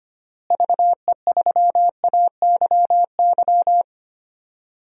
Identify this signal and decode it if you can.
Morse code, heard as a keyed tone, about 25 wpm, 700 Hz. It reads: VE3AYY
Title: DXCC-100 Call Signs - 6 Characters - Repeated 3x - 25wpm